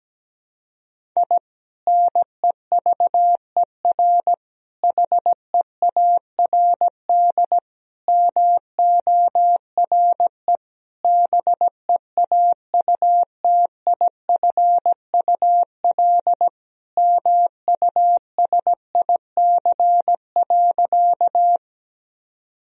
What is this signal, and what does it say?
Morse code, heard as a keyed tone, about 17 wpm, 700 Hz. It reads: I NEVER HEARD MORE BEAUTIFUL MUSIC.
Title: Sentences from Top 1000 Words - Review Part 1 - 17wpm